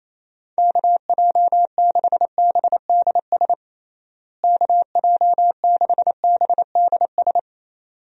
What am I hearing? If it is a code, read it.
Morse code, heard as a keyed tone, about 28 wpm, 700 Hz. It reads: KJ6BDS KJ6BDS
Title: USA Call Signs - 6 Characters - Repeated 2x - 28wpm